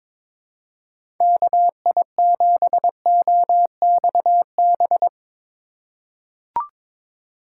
Morse code, heard as a keyed tone, about 22 wpm, 700 Hz. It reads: KI7OXB
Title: USA Call Signs - 6 Characters - 4x Character Spacing - 22wpm